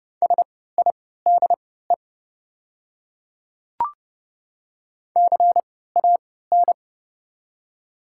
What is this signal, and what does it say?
Morse code, heard as a keyed tone, about 30 wpm, spaced as for 17 wpm, 700 Hz. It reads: SIDE CAN
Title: 3x Character Spacing - Top 100 Words - 30wpm